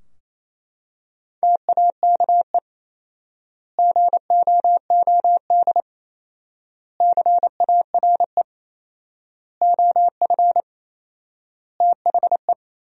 Morse code, heard as a keyed tone, about 28 wpm, 700 Hz. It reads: TAKE GOOD CARE OF THE
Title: 4x Word-Spacing - Sentences from Top 400 Words - 28wpm